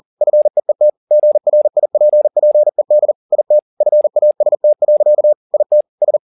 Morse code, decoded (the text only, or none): FEET GRIPPED IT FAST. IT S